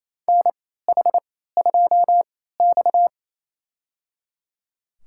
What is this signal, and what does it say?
Morse code, heard as a keyed tone, about 28 wpm, spaced as for 16 wpm, 700 Hz. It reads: NH2X
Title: USA Call Signs - 4 Characters - 3x Character Spacing - 28wpm